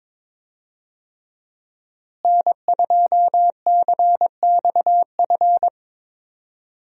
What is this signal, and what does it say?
Morse code, heard as a keyed tone, about 22 wpm, 700 Hz. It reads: N2CXF
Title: USA Call Signs - 5 Characters - Repeated 3x - 22wpm